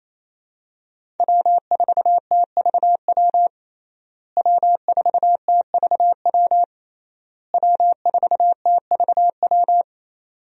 Morse code, heard as a keyed tone, about 28 wpm, 700 Hz. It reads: W4TVW W4TVW W4TVW
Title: USA Call Signs - 5 Characters - Repeated 3x - 28wpm